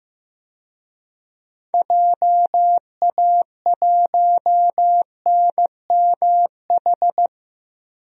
Morse code, heard as a keyed tone, about 15 wpm, 700 Hz. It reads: JA1NMH
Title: DXCC-100 Call Signs - 6 Characters - 15wpm